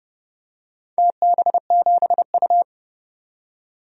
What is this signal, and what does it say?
Morse code, heard as a keyed tone, about 30 wpm, 700 Hz. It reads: TB7U
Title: DXCC-100 Call Signs - 4 Characters - 30wpm